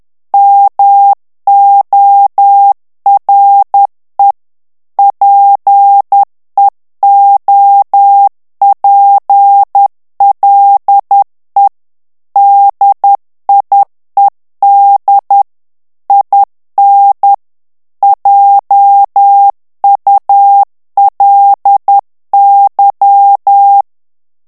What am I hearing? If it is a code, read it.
Morse code, heard as a keyed tone, about 10 wpm, 800 Hz. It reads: MORE PEOPLE DIED IN JULY
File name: morse mp3 / ANGLAIS texte clair 2 vitesse 10 mots minute